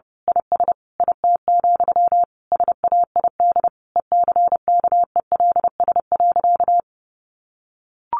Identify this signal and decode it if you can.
Morse code, heard as a keyed tone, about 30 wpm, 700 Hz. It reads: IS IT, SAID ECKELS.
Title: A Sound Of Thunder 30wpm